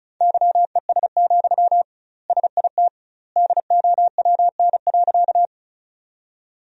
Morse code, heard as a keyed tone, about 36 wpm, 700 Hz. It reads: YES, SIT DOWN.
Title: 2x Word-Spacing - Sentences from Top 500 Words - 35wpm